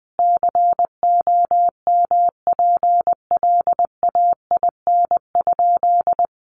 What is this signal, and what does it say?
Morse code, heard as a keyed tone, about 20 wpm, 700 Hz. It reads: COMPLAIN?
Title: Sentences from Top 1300 Words 20wpm